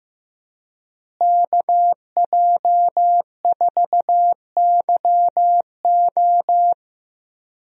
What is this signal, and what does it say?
Morse code, heard as a keyed tone, about 15 wpm, 700 Hz. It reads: KJ4YO
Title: USA Call Signs - 5 Characters - Repeated 2x - 15wpm